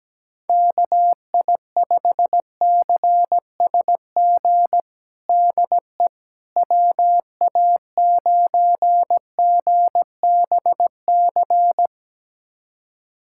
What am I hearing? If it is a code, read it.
Morse code, heard as a keyed tone, about 17 wpm, 700 Hz. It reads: KI5CSG DE WA9GBC
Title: USA Callsign DE Callsign - 6 Characters - 17wpm